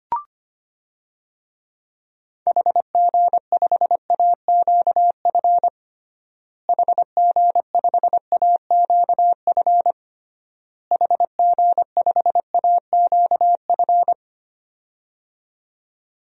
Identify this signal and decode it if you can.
Morse code, heard as a keyed tone, about 25 wpm, 700 Hz. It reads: HG5AQF HG5AQF HG5AQF
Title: DXCC-100 Call Signs - 6 Characters - Repeated 3x - 25wpm